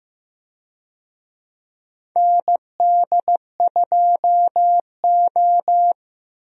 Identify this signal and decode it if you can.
Morse code, heard as a keyed tone, about 15 wpm, 700 Hz. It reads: ND2O
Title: USA Call Signs - 4 Characters - Repeated 2x - 15wpm